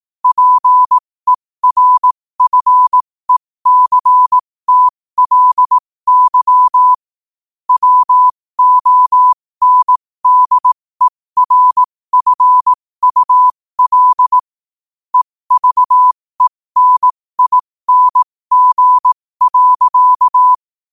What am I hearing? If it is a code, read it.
Morse code, heard as a keyed tone, about 18 wpm, spaced as for 15 wpm, 1000 Hz. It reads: PERFECTLY WONDERFUL EVENING.
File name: qotdfiles / 15wpm